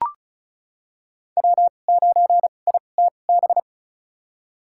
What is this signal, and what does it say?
Morse code, heard as a keyed tone, about 36 wpm, spaced as for 25 wpm, 700 Hz. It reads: W9ITB
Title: USA Call Signs - 5 Characters - 2x Character Spacing - 35wpm